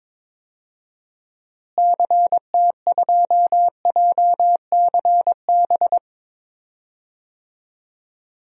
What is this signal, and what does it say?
Morse code, heard as a keyed tone, about 22 wpm, 700 Hz. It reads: CT2JCB
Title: DXCC-100 Call Signs - 6 Characters - Repeated 2x - 22wpm